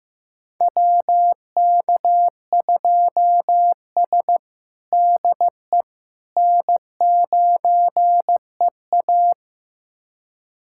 Morse code, heard as a keyed tone, about 16 wpm, 700 Hz. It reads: WK2S DE N9EA
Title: USA Callsign DE Callsign - 4 Characters - 15wpm